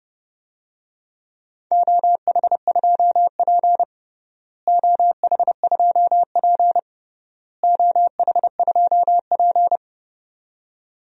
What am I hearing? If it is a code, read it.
Morse code, heard as a keyed tone, about 30 wpm, 700 Hz. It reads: OH2P OH2P OH2P
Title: DXCC-100 Call Signs - 4 Characters - Repeated 3x - 30wpm